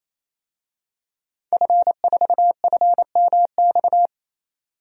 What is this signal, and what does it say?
Morse code, heard as a keyed tone, about 27 wpm, 700 Hz. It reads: F4FMX
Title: DXCC-100 Call Signs - 5 Characters - 28wpm